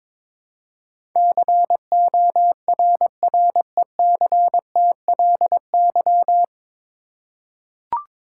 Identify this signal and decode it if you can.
Morse code, heard as a keyed tone, about 22 wpm, 700 Hz. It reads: CORRECTLY
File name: 9 Letter Words - First 3 Letters Given - 22wpm